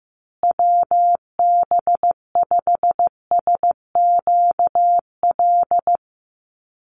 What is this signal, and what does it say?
Morse code, heard as a keyed tone, about 15 wpm, 700 Hz. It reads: WB5SQL
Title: USA Call Signs Repeated 3x 15wpm